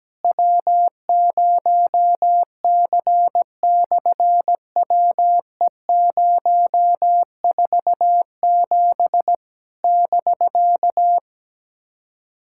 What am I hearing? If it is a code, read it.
Morse code, heard as a keyed tone, about 17 wpm, 700 Hz. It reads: W0C/WE047 <BK>
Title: SOTA QSO Practice 17wpm